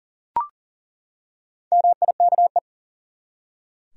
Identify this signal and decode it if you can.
Morse code, heard as a keyed tone, about 40 wpm, 700 Hz. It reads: MIKE
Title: USA Common Names - Max 4 Letters - 40wpm